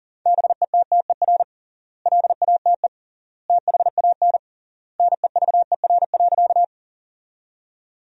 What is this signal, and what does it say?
Morse code, heard as a keyed tone, about 40 wpm, 700 Hz. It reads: BETTER LATE THAN NEVER.
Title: 3x Word-Spacing - Sentences from Top 400 Words - 40wpm